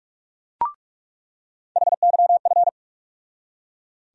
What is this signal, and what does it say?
Morse code, heard as a keyed tone, about 43 wpm, 700 Hz. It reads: HYF
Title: USA Call Sign Postfixes - 3 Letters 45wpm